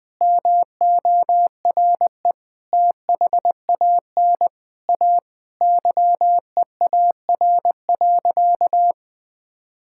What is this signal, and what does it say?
Morse code, heard as a keyed tone, about 20 wpm, 700 Hz. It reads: MORE THAN A YEAR.
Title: Sentences with 5-Word Phrases - Spoken Beginning - 20wpm